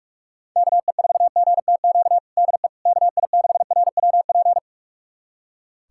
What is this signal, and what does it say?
Morse code, heard as a keyed tone, about 46 wpm, 700 Hz. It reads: KE4GTQ DE KI6RWP
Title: USA Callsign DE Callsign - 6 Characters - 45wpm